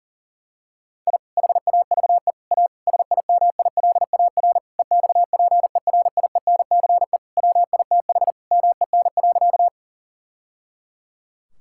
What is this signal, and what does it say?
Morse code, heard as a keyed tone, about 40 wpm, 700 Hz. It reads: I HAVE A SIMILAR EXPERIENCE WITH MEN.
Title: Sentences from Top 1000 Words - Review Part 5 - 40wpm